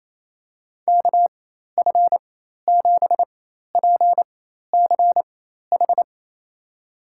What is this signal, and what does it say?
Morse code, heard as a keyed tone, about 28 wpm, spaced as for 13 wpm, 700 Hz. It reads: KF7PCH
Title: USA Call Signs - 6 Characters - 4x Character Spacing - 28wpm